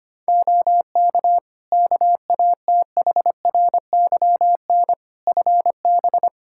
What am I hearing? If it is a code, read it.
Morse code, heard as a keyed tone, about 25 wpm, 700 Hz. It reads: OK KATHRYN FB
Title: K1USN SST QSO Practice 25wpm